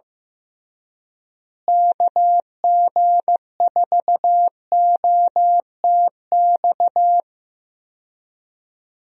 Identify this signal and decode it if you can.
Morse code, heard as a keyed tone, about 16 wpm, 700 Hz. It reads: KG4OTX
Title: USA Call Signs - 6 Characters - Repeated 3x - 15wpm